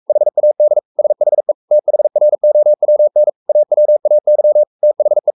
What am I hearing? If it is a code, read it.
Morse code, heard as a keyed tone, about 44 wpm, 600 Hz. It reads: HAD SHE THROWN AWAY THE
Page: Romanen Never af Ken Follet på engelsk.